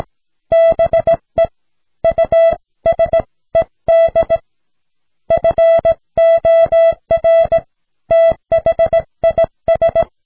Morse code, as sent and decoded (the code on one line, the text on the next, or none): -... / ..- ... -.. / ..-. --- .-. / - .... .. ...
BE USED FOR THIS